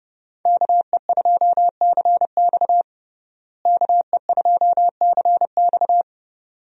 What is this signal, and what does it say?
Morse code, heard as a keyed tone, about 32 wpm, 700 Hz. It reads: KE2CX KE2CX
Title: USA Call Signs - 5 Characters - Repeated 2x - 30wpm